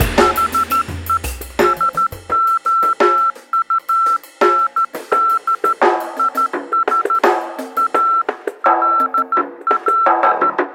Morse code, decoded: HE SOUND IS ALL